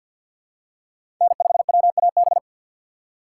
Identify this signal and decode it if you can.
Morse code, heard as a keyed tone, about 53 wpm, 700 Hz. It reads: N5WAB
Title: USA Call Signs - 3x Character Spacing - 50wpm